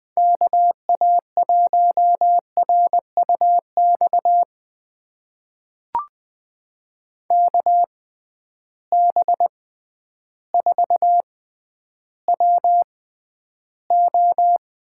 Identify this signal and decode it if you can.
Morse code, heard as a keyed tone, about 21 wpm, spaced as for 11 wpm, 700 Hz. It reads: KA1RUX KB4WO
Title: USA Call Signs - 6 Characters - 6x Character Spacing - 20wpm